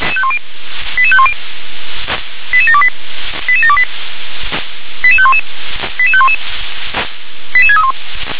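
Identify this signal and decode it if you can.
Five-tone selective call (ZVEI), heard as quick runs of five tones, tones of about 70 ms. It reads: 88510 88728 88728 88511 88511 88631
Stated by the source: ZVEI five-tone selective call received by SDR.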